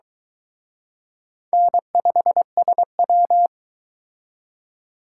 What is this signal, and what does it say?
Morse code, heard as a keyed tone, about 22 wpm, 700 Hz. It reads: N5SW